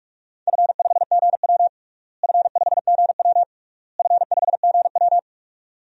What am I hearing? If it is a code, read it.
Morse code, heard as a keyed tone, about 46 wpm, 700 Hz. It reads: F5GW F5GW F5GW
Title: DXCC-100 Call Signs - 4 Characters - Repeated 3x - 45wpm